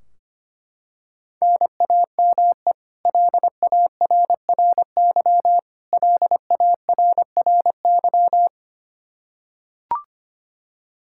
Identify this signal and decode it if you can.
Morse code, heard as a keyed tone, about 25 wpm, 700 Hz. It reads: NAME LARRY LARRY
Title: Practice QSO - Exchange 1 - Beginning - 25wpm